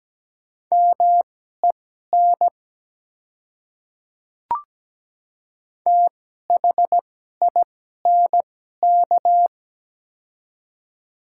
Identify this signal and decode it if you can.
Morse code, heard as a keyed tone, about 17 wpm, spaced as for 12 wpm, 700 Hz. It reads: MEN THINK
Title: 2x Character Spacing - Top 200 Words - 17wpm